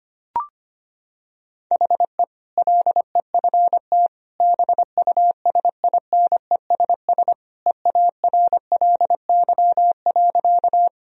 Morse code, heard as a keyed tone, about 25 wpm, 700 Hz. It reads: HE LEFT BUSINESS EARLY.
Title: Sentences from Top 1000 Words - Review Part 2 - 25wpm